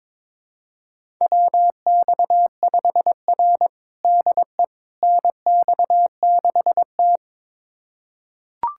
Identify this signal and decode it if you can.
Morse code, heard as a keyed tone, about 22 wpm, 700 Hz. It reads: WX5R DE NX6T
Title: USA Callsign DE Callsign - 4 Characters - 22wpm